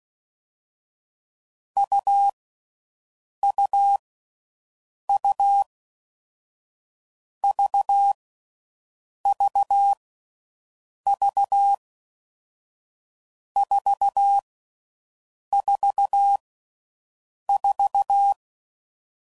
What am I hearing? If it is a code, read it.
Morse code, heard as a keyed tone, about 16 wpm, 800 Hz. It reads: U U U V V V 4 4 4
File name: Combinations / Difficult Combos / AUV AUV4